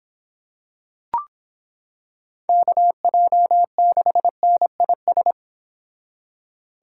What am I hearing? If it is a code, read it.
Morse code, heard as a keyed tone, about 25 wpm, 700 Hz. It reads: KJ6NIS